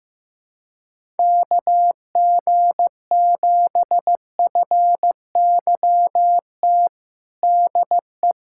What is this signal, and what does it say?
Morse code, heard as a keyed tone, about 15 wpm, 700 Hz. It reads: KG7FYT DE
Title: USA Callsign DE Callsign - 6 Characters - 15wpm